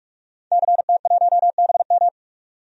Morse code, heard as a keyed tone, about 41 wpm, 700 Hz. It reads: CT1BM
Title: DXCC-100 Call Signs - 5 Characters - Repeated 2x - 45wpm